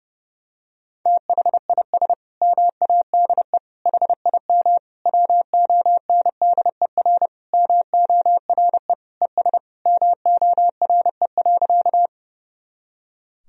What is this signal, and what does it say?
Morse code, heard as a keyed tone, about 30 wpm, 700 Hz. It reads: THIS MADE HIM WONDER MORE ES MORE.
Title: Sentences from Top 500 Words and Abbreviations 30wpm